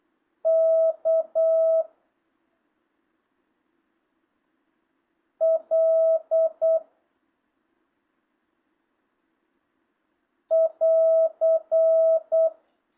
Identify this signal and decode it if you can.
Morse code, heard as a keyed tone, about 8 wpm, 650 Hz. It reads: K L <AR>
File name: jh8cbh / 2025cw